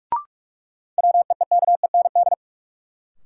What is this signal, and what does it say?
Morse code, heard as a keyed tone, about 45 wpm, 700 Hz. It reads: WEEKEND
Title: Sets of 5 Words - New Word Drill - 45wpm